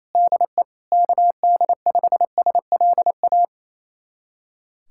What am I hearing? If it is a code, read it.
Morse code, heard as a keyed tone, about 28 wpm, 700 Hz. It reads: DE KD5SLA
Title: USA Callsign DE Callsign - 6 Characters - 28wpm